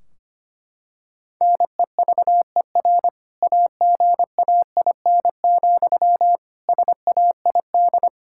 Morse code, heard as a keyed tone, about 25 wpm, 700 Hz. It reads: NEVER AGAIN, SAID
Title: Sentences from Top 1600 Words 25wpm